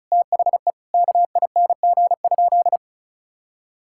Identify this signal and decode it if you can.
Morse code, heard as a keyed tone, about 36 wpm, 700 Hz. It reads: THE KING?
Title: Sentences from Top 1600 Words 35wpm